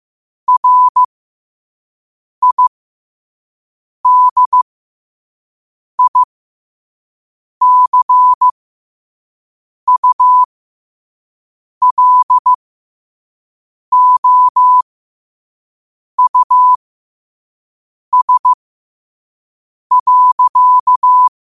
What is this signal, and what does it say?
Morse code, heard as a keyed tone, about 15 wpm, spaced as for 5 wpm, 1000 Hz. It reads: RIDICULOUS.
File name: qotdfiles / 5wpm